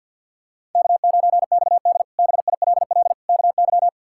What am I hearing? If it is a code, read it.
Morse code, heard as a keyed tone, about 46 wpm, 700 Hz. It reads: K9XD BILL KY
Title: K1USN SST QSO Practice - 50wpm